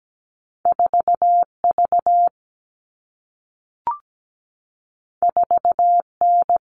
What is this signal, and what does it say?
Morse code, heard as a keyed tone, about 17 wpm, 700 Hz. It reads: 4V 4N
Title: All Cut Numbers - 2-Digit 17wpm